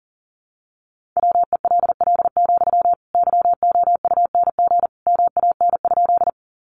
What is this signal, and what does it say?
Morse code, heard as a keyed tone, about 40 wpm, 700 Hz. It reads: WELL, YOUNG MAN?
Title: Sentences from Top 400 Words 40wpm